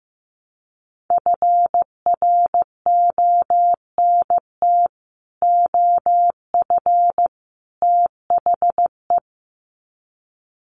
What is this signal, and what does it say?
Morse code, heard as a keyed tone, about 15 wpm, 700 Hz. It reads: FRONT OF THE
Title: Sets of 3 Words 15wpm